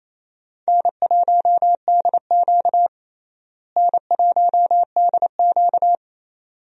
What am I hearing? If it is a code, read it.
Morse code, heard as a keyed tone, about 28 wpm, 700 Hz. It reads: N1DQ N1DQ
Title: USA Call Signs - 4 Characters - Repeated 2x - 28wpm